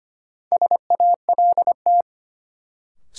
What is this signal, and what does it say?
Morse code, heard as a keyed tone, about 25 wpm, 700 Hz. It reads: SALT
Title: Sentences from Top 800 Words - New Word Drill - 25wpm